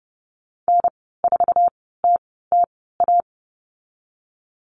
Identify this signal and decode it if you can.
Morse code, heard as a keyed tone, about 31 wpm, spaced as for 17 wpm, 690 Hz. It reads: N4TTA
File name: Lesson 15g - Number 5 - Call Signs 30wpm